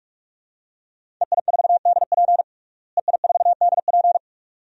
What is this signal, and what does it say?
Morse code, heard as a keyed tone, about 46 wpm, 700 Hz. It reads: EI4DP EI4DP
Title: DXCC-100 Call Signs - 5 Characters - Repeated 2x - 45wpm